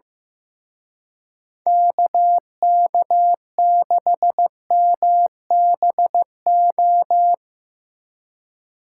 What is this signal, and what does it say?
Morse code, heard as a keyed tone, about 17 wpm, 700 Hz. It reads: KK6MBO
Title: Speed Racing - USA 2x3 Call Signs 15wpm